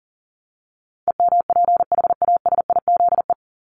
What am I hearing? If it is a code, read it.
Morse code, heard as a keyed tone, about 40 wpm, 700 Hz. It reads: EMPHASIZE
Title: Top 1251-1500 Words 40wpm